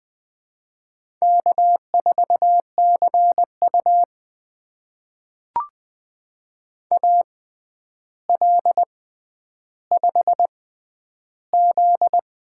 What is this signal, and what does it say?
Morse code, heard as a keyed tone, about 20 wpm, spaced as for 10 wpm, 700 Hz. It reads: K4CU AL5Z
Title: USA Call Signs - 4 Characters - 6x Character Spacing - 20wpm